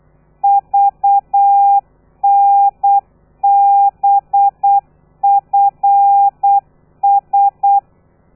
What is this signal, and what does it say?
Morse code, heard as a keyed tone, about 8 wpm, 800 Hz. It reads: VNBFS
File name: telegrafia / i2tfj / Lez13 35